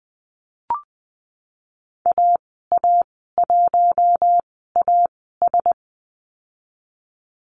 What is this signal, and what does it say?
Morse code, heard as a keyed tone, about 20 wpm, spaced as for 15 wpm, 700 Hz. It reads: AA1AS
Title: 062 Lesson 11g - Letter h - Call Signs 20wpm